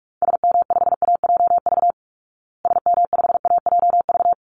Morse code, heard as a keyed tone, about 44 wpm, 700 Hz. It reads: SM5AJV SM5AJV
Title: Call Signs Repeated 3x - 45wpm